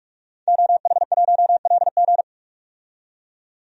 Morse code, heard as a keyed tone, about 47 wpm, 700 Hz. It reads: OH1LG
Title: DXCC-100 Call Signs - 5 Characters - 45wpm